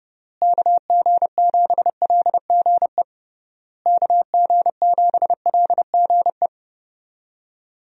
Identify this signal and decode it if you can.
Morse code, heard as a keyed tone, about 32 wpm, 700 Hz. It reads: KG7LGE KG7LGE
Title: USA Call Signs - 6 Characters - Repeated 3x - 30wpm